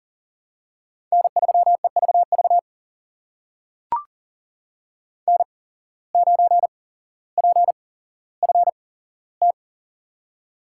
Morse code, heard as a keyed tone, about 40 wpm, spaced as for 17 wpm, 700 Hz. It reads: N3EVV N9PFT
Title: USA Call Signs - 5 Characters - 8x Character Spacing - 40wpm